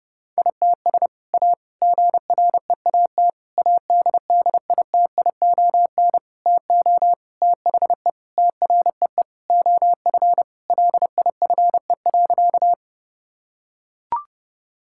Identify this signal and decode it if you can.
Morse code, heard as a keyed tone, about 30 wpm, 700 Hz. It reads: ITS A GREAT ADDITION TO THE TREE OF LIFE.
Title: Sentences from Top 1000 Words - Review Part 6 - 30wpm